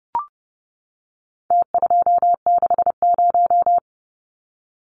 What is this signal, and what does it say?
Morse code, heard as a keyed tone, about 30 wpm, 700 Hz. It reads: T260
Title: Numbers - Contest Style - 4-Digit 30wpm